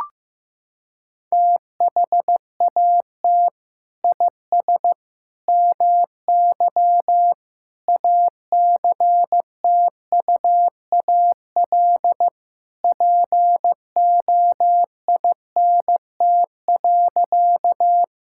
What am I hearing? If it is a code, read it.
Morse code, heard as a keyed tone, about 15 wpm, 700 Hz. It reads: THAT IS MY ACTUAL POINT.
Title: Sentences from Top 1000 Words - Review Part 5 - 15wpm